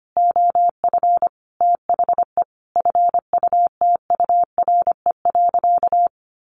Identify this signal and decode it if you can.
Morse code, heard as a keyed tone, about 25 wpm, 700 Hz. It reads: OF THE FUTURE.
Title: Sentences from Top 700 Words 25wpm